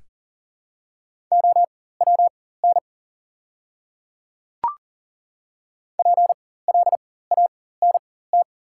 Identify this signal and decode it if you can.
Morse code, heard as a keyed tone, about 40 wpm, spaced as for 19 wpm, 700 Hz. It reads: OWN PLANT
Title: 4x Character Spacing - Top 500 Words - 40wpm